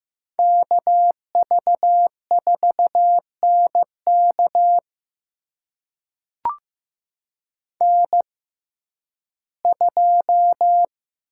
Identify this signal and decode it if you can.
Morse code, heard as a keyed tone, about 15 wpm, spaced as for 11 wpm, 700 Hz. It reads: KV4NK N2
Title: USA Call Signs - 5 Characters - 6x Character Spacing - 15wpm